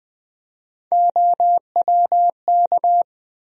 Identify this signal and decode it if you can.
Morse code, heard as a keyed tone, about 20 wpm, 700 Hz. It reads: OWK
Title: USA Call Sign Postfixes - 3 Letters 20wpm